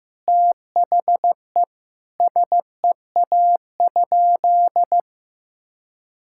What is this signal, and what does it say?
Morse code, heard as a keyed tone, about 14 wpm, 700 Hz. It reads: THE SEA?
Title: Sentences from Top 1000 Words - Review Part 6 - 15wpm